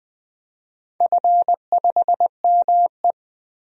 Morse code, heard as a keyed tone, about 20 wpm, 700 Hz. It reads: F5ME
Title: DXCC-100 Call Signs - 4 Characters - 20wpm